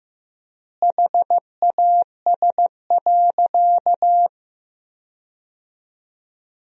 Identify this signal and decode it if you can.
Morse code, heard as a keyed tone, about 16 wpm, 700 Hz. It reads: HAS.
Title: 3x Word-Spacing - Sentences from Top 500 Words - 15wpm